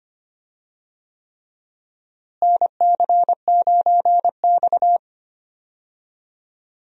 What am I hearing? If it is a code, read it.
Morse code, heard as a keyed tone, about 25 wpm, 700 Hz. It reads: NC9X
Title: USA Call Signs - 4 Characters - Repeated 3x - 25wpm